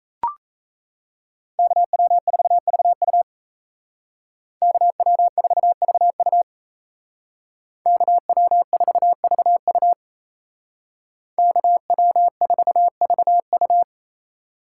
Morse code, heard as a keyed tone, about 33 wpm, 700 Hz. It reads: KW4VU KW4VU KW4VU KW4VU
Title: Speed Racing - USA 2x2 Call Signs 28wpm